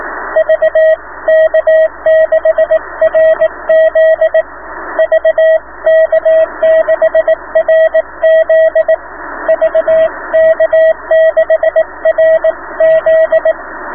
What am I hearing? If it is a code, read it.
Morse code, heard as a keyed tone, about 15 wpm, 690 Hz, with radio noise. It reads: VK6RZ VK6RZ VK6RZ